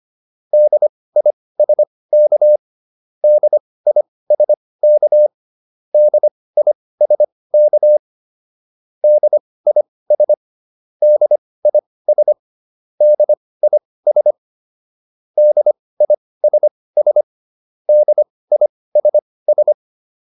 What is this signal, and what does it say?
Morse code, heard as a keyed tone, about 25 wpm, spaced as for 18 wpm, 600 Hz. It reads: DISK DISK DISK DIS DIS DIS DISS DISS